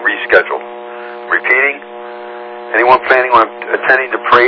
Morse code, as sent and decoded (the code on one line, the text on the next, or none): none